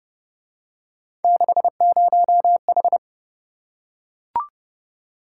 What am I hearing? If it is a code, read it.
Morse code, heard as a keyed tone, about 26 wpm, 700 Hz. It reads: 60H